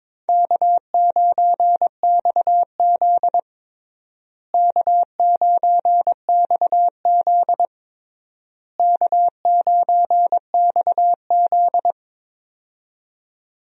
Morse code, heard as a keyed tone, about 22 wpm, 700 Hz. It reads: K9XZ K9XZ K9XZ
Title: USA Call Signs - 4 Characters - Repeated 3x - 22wpm